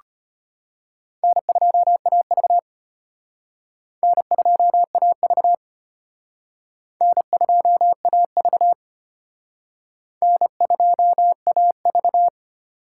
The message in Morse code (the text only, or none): N2AV N2AV N2AV N2AV